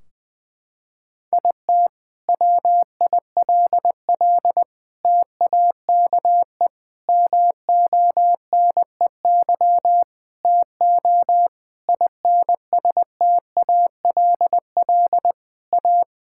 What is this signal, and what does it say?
Morse code, heard as a keyed tone, about 21 wpm, 700 Hz. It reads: IT WILL TAKE MONEY TO INSTALL A
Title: Sentences from Top 1600 Words 20wpm